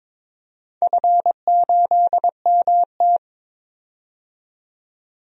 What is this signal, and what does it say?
Morse code, heard as a keyed tone, about 22 wpm, 700 Hz. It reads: F8MT
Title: DXCC-100 Call Signs - 4 Characters - 22wpm